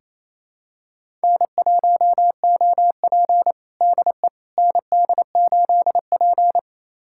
Morse code, heard as a keyed tone, about 28 wpm, 700 Hz. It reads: N1OP DE ND8P
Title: USA Callsign DE Callsign - 4 Characters - 28wpm